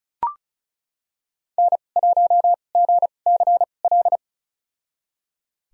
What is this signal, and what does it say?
Morse code, heard as a keyed tone, about 36 wpm, spaced as for 25 wpm, 700 Hz. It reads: N1GCL
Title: USA Call Signs - 2x Character Spacing - 35wpm